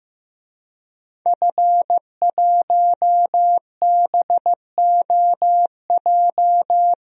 Morse code, heard as a keyed tone, about 15 wpm, 700 Hz. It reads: F1BOJ
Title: DXCC-100 Call Signs - 5 Characters - Repeated 3x - 15wpm